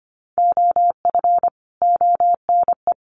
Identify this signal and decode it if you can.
Morse code, heard as a keyed tone, about 25 wpm, 700 Hz. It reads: OF ONE
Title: Sets of 4 Words 25wpm